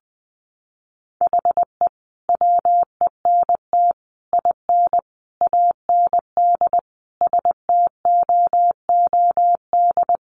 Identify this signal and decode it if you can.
Morse code, heard as a keyed tone, about 20 wpm, 700 Hz. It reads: HE WENT IN AND STOOD
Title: Sentences from Top 1300 Words 20wpm